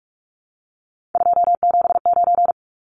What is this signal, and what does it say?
Morse code, heard as a keyed tone, about 44 wpm, 700 Hz. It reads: 279
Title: Numbers - 3-Digit 45wpm